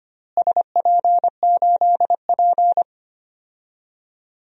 Morse code, heard as a keyed tone, about 25 wpm, 700 Hz. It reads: SP8P
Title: DXCC-100 Call Signs - 4 Characters - Repeated 2x - 25wpm